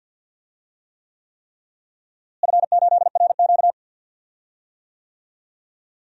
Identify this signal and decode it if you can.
Morse code, heard as a keyed tone, about 50 wpm, 700 Hz. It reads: F8RQ
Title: DXCC-100 Call Signs - 4 Characters - Repeated 3x - 50wpm